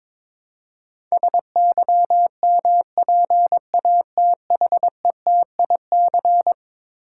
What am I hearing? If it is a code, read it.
Morse code, heard as a keyed tone, about 22 wpm, 700 Hz. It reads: SYMPATHETIC
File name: practice / 22wpm / Eleven Letter Words - 22wpm